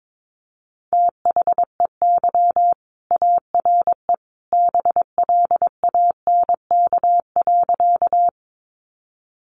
Morse code, heard as a keyed tone, about 22 wpm, 700 Hz. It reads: THEY ARE BLANK.
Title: Sentences from Top 1500 Words 22wpm